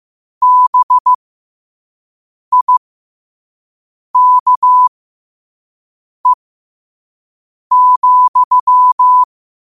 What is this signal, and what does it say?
Morse code, heard as a keyed tone, about 15 wpm, spaced as for 5 wpm, 1000 Hz. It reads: BIKE,
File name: qotdfiles / 5wpm